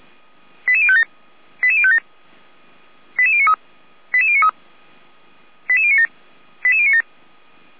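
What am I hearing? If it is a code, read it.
Five-tone selective call (ZVEI), heard as quick runs of five tones, tones of about 70 ms. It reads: 88057 88057 88093 88093 88097 88097
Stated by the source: ZVEI five-tone call 88053 received with an SDR, recorded on the Feuerwehr Orlamünde server.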